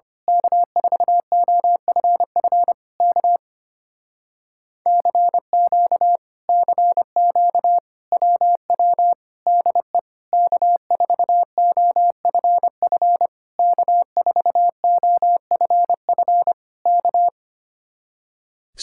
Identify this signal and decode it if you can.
Morse code, heard as a keyed tone, about 24 wpm, 700 Hz. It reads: K4OFF K CQ CQ WW DE K4OFF K4OFF K
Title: Speed-Racing Common QSO Phrases - 25wpm